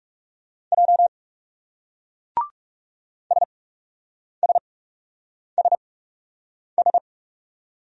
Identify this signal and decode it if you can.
Morse code, heard as a keyed tone, about 40 wpm, spaced as for 8 wpm, 700 Hz. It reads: J SSSS